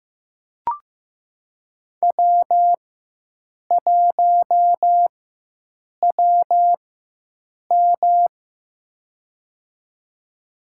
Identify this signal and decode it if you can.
Morse code, heard as a keyed tone, about 15 wpm, spaced as for 7 wpm, 700 Hz. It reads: W1WM
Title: USA Call Signs - 4 Characters - 4x Character Spacing - 15wpm